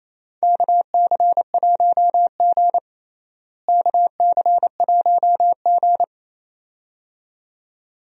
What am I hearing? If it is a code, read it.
Morse code, heard as a keyed tone, about 28 wpm, 700 Hz. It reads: KC1G KC1G
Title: USA Call Signs - 4 Characters - Repeated 2x - 28wpm